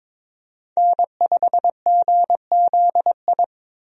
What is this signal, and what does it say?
Morse code, heard as a keyed tone, about 22 wpm, 700 Hz. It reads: N5GZI
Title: USA Call Signs - 5 Characters - 22wpm